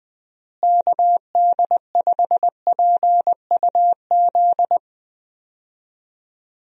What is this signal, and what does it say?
Morse code, heard as a keyed tone, about 21 wpm, 700 Hz. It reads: KD5PUZ
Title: USA Call Signs - 6 Characters - Repeated 2x - 20wpm